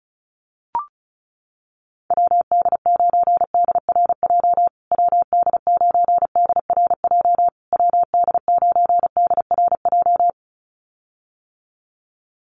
Morse code, heard as a keyed tone, about 34 wpm, 700 Hz. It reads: WD9DRJ WD9DRJ WD9DRJ
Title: USA Call Signs Repeated 3x 35wpm